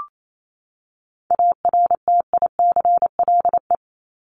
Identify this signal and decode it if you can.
Morse code, heard as a keyed tone, about 28 wpm, 700 Hz. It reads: ARTICLE
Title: Top 751-1000 Words 28wpm